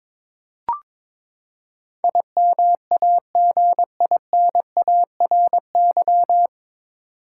Morse code, heard as a keyed tone, about 21 wpm, 700 Hz. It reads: IMAGINARY
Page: Morse Code Ninja practice 22wpm